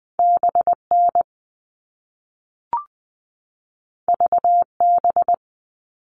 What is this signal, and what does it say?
Morse code, heard as a keyed tone, about 20 wpm, 700 Hz. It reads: BN VB
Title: All Cut Numbers - 2-Digit 20wpm